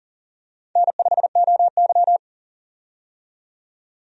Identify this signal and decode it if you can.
Morse code, heard as a keyed tone, about 40 wpm, 700 Hz. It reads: N5OY
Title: USA Call Signs - 4 Characters - 40wpm